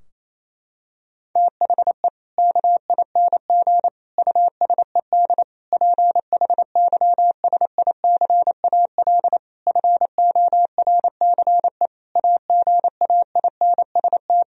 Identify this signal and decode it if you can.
Morse code, heard as a keyed tone, about 28 wpm, 700 Hz. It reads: THE KING USED PHYSICAL FORCE AGAINST
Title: Sentences from Top 1000 Words - Review Part 4 - 28wpm